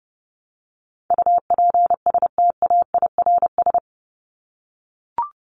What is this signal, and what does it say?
Morse code, heard as a keyed tone, about 30 wpm, 700 Hz. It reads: UPSTAIRS
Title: Top 1751-2000 Words 30wpm